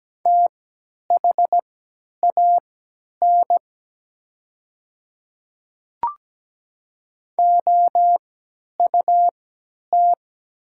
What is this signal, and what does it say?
Morse code, heard as a keyed tone, about 17 wpm, spaced as for 10 wpm, 700 Hz. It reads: THAN OUT
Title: 3x Character Spacing - Top 100 Words - 17wpm